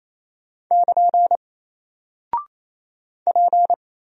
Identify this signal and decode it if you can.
Morse code, heard as a keyed tone, about 27 wpm, 700 Hz. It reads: <KN> P
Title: Single Character (Letter:Number:Punctuation:Prosign) 28wpm